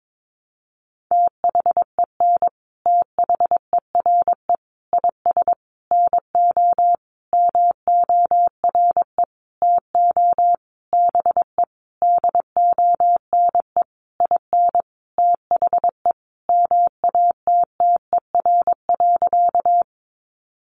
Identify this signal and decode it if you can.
Morse code, heard as a keyed tone, about 22 wpm, 700 Hz. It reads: THEN THERE IS NO MORE TO BE DONE IN THE MATTER.
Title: Sentences from Top 700 Words 22wpm